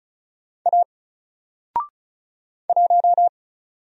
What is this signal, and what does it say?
Morse code, heard as a keyed tone, about 36 wpm, 700 Hz. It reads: A 1
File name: Single-Letter-Number-35wpm